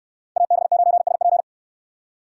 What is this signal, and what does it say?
Morse code, heard as a keyed tone, about 69 wpm, 700 Hz. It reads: AB9SP